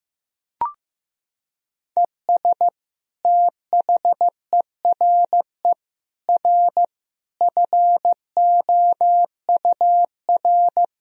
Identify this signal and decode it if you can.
Morse code, heard as a keyed tone, about 15 wpm, 700 Hz. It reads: ES THERE R FOUR
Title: Sentences from Top 300 Words and Abbreviations 15wpm